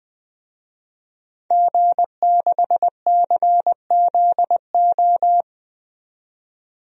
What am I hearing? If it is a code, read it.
Morse code, heard as a keyed tone, about 20 wpm, 700 Hz. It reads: G6CZO
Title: DXCC-100 Call Signs - 5 Characters - 20wpm